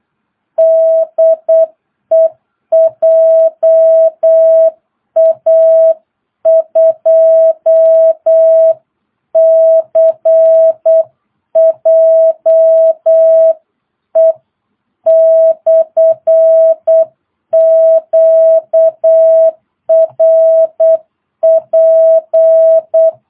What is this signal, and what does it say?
Morse code, heard as a keyed tone, about 8 wpm, 650 Hz. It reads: DE JA2CJE/QRP
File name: jh8cbh / 2022cw